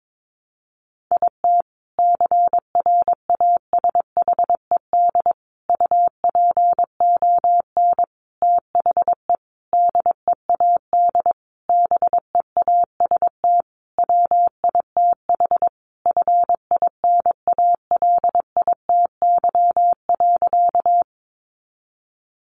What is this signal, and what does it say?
Morse code, heard as a keyed tone, about 22 wpm, 700 Hz. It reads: IT CRASHED UPON THE DEAD BEAST WITH FINALITY.
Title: A Sound Of Thunder 22wpm